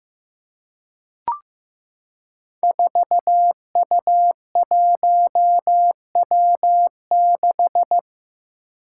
Morse code, heard as a keyed tone, about 15 wpm, 700 Hz. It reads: 4U1W6
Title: Five Challenging Letter-Number Combinations 15wpm